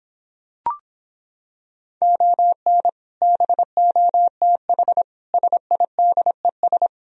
Morse code, heard as a keyed tone, about 26 wpm, 700 Hz. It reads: ON BOTH SIDES